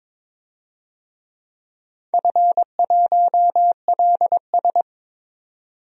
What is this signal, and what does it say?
Morse code, heard as a keyed tone, about 22 wpm, 700 Hz. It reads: F1LS
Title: DXCC-100 Call Signs - 4 Characters - 22wpm